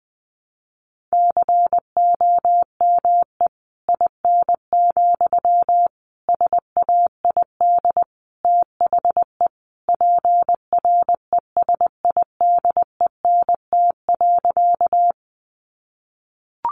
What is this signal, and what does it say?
Morse code, heard as a keyed tone, about 20 wpm, 700 Hz. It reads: COME IN, SAID THE PRESIDENT.
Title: Sentences from Top 1000 Words 20wpm